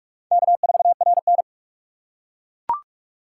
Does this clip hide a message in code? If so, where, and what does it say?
Yes: Morse code from 0.3 to 1.4 s, heard as a keyed tone, about 40 wpm, 700 Hz. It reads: K4RN